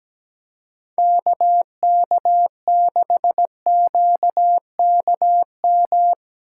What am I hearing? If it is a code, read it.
Morse code, heard as a keyed tone, about 17 wpm, 700 Hz. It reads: KK6QKM
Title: USA Call Signs - 6 Characters - 6x Character Spacing - 17wpm